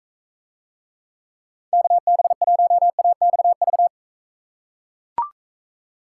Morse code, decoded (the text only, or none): KB1AXV